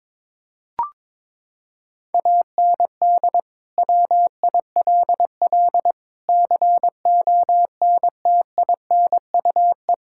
Morse code, heard as a keyed tone, about 22 wpm, 700 Hz. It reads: AND WILL CONTINUE
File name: practice / 22wpm / Sets of 6 Words - 22wpm